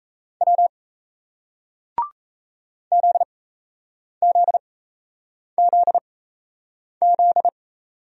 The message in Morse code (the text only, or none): W ZZZZ